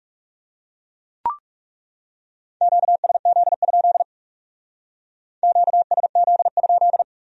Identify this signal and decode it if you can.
Morse code, heard as a keyed tone, about 40 wpm, 700 Hz. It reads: QSZ? QSZ?